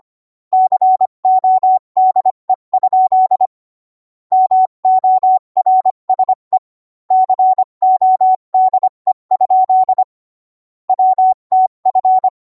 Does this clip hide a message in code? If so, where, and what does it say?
Yes: Morse code from 0.5 to 12.3 s, heard as a keyed tone, about 25 wpm, 750 Hz. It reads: CODE? MORSE CODE? WTF